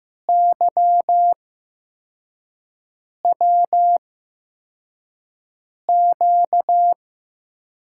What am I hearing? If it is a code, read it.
Morse code, heard as a keyed tone, about 15 wpm, spaced as for 4 wpm, 700 Hz. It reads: YWQ
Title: USA Call Signs - 6 Characters - 8x Character Spacing - 15wpm